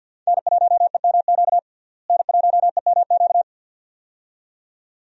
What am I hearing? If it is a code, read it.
Morse code, heard as a keyed tone, about 51 wpm, 700 Hz. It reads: N1EMQ N1EMQ
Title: USA Call Signs - 5 Characters - Repeated 2x - 50wpm